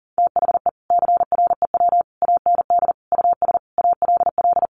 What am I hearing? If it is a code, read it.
Morse code, heard as a keyed tone, about 40 wpm, 700 Hz. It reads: THE CREW AND US ALL
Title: Sentences from Top 1500 Words 40wpm